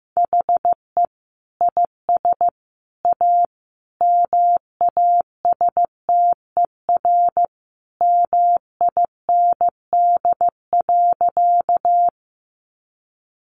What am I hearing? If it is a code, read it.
Morse code, heard as a keyed tone, about 14 wpm, 700 Hz. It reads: HE IS A MASTER MIND.
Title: Sentences from Top 1300 Words 15wpm